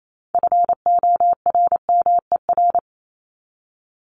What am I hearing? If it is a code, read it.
Morse code, heard as a keyed tone, about 28 wpm, 700 Hz. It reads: FORMER
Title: Top 1751-2000 Words 28wpm